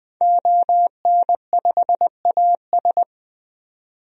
Morse code, heard as a keyed tone, about 20 wpm, 700 Hz. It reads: ON5AS
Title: DXCC-100 Call Signs - 5 Characters - 20wpm